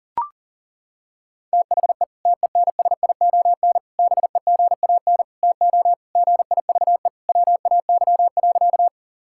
Morse code, heard as a keyed tone, about 40 wpm, 700 Hz. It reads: THE TENSION BEGAN TO GIVE WAY.
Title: Sentences from Top 1600 Words 40wpm